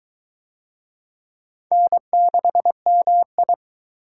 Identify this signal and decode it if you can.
Morse code, heard as a keyed tone, about 20 wpm, 700 Hz. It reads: N6MI